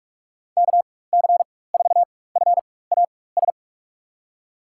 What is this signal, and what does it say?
Morse code, heard as a keyed tone, about 47 wpm, spaced as for 21 wpm, 700 Hz. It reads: KC4FAS
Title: USA Call Signs - 6 Characters - 4x Character Spacing - 45wpm